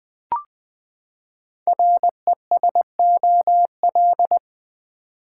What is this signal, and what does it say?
Morse code, heard as a keyed tone, about 20 wpm, 700 Hz. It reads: RESOL
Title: Build A Word - 5 Letters At A Time - 10-Letter Words - 20wpm